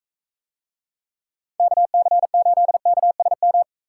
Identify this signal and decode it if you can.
Morse code, heard as a keyed tone, about 40 wpm, 700 Hz. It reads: KC8KSM